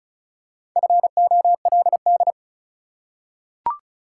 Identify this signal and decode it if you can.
Morse code, heard as a keyed tone, about 36 wpm, 700 Hz. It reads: FOLD
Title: Four Letter Words - 35wpm